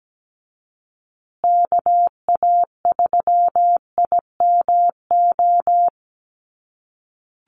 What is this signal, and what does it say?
Morse code, heard as a keyed tone, about 17 wpm, 700 Hz. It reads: KA3IMO
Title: USA Call Signs Repeated 3x 17wpm